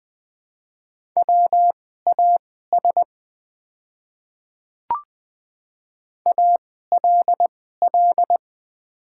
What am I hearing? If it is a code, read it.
Morse code, heard as a keyed tone, about 20 wpm, spaced as for 15 wpm, 700 Hz. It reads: WAS ALL
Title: 2x Character Spacing - Top 200 Words - 20wpm